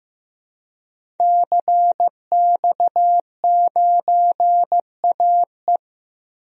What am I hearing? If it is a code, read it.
Morse code, heard as a keyed tone, about 15 wpm, 700 Hz. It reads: CX9AE
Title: DXCC-100 Call Signs - 5 Characters - Repeated 2x - 15wpm